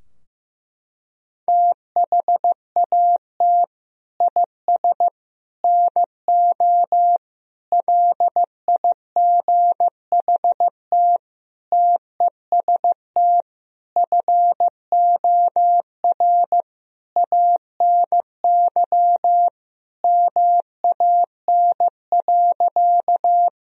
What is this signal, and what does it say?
Morse code, heard as a keyed tone, about 15 wpm, 700 Hz. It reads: THAT IS NO LIGHT TEST FOR ANY MAN.
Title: Sentences from Top 1000 Words - Review Part 3 - 15wpm